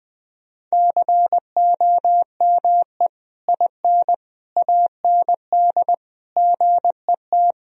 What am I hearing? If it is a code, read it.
Morse code, heard as a keyed tone, about 20 wpm, 700 Hz. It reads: COME IN AND GET
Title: Sentences from Top 1000 Words - Review Part 2 - 20wpm